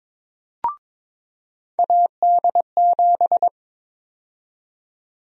Morse code, heard as a keyed tone, about 22 wpm, 700 Hz. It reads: AD7
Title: USA Call Sign Prefixes - 22wpm